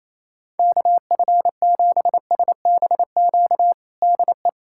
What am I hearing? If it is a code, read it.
Morse code, heard as a keyed tone, about 28 wpm, 700 Hz. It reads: KF7SBQ DE
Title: USA Callsign DE Callsign 28wpm